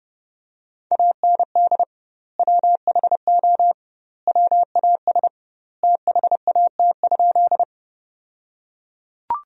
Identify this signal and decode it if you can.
Morse code, heard as a keyed tone, about 30 wpm, 700 Hz. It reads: AND WHO WAS THAT?
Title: 2x Word-Spacing - Sentences from Top 100 Words - 30wpm